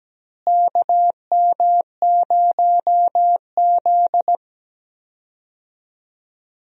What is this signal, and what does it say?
Morse code, heard as a keyed tone, about 17 wpm, 700 Hz. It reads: KM0Z
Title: USA Call Signs - 4 Characters - Repeated 3x - 17wpm